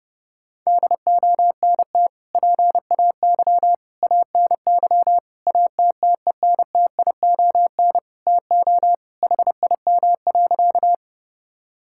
Morse code, heard as a keyed tone, about 30 wpm, 700 Hz. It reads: DONT PAY ANY ATTENTION TO HIM.
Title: Sentences from Top 1000 Words - Review Part 6 - 30wpm